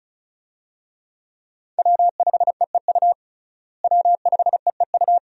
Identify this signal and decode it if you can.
Morse code, heard as a keyed tone, about 34 wpm, 700 Hz. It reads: W5EEU W5EEU
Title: USA Call Signs - 5 Characters - Repeated 2x - 35wpm